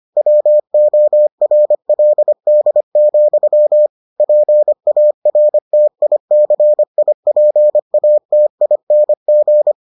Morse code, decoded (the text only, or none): WORLD, PARTICIPATING